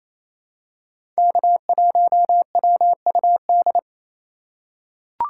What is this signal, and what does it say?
Morse code, heard as a keyed tone, about 28 wpm, 700 Hz. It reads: K1WUD
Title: USA Call Signs - 2x Character Spacing - 28wpm